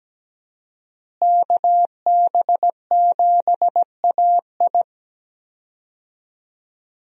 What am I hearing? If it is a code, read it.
Morse code, heard as a keyed tone, about 17 wpm, 700 Hz. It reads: KB7AI
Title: USA Call Signs - 5 Characters - Repeated 3x - 17wpm